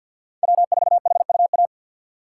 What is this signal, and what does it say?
Morse code, heard as a keyed tone, about 51 wpm, 700 Hz. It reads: W4HUA
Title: USA Call Signs - 3x Character Spacing - 50wpm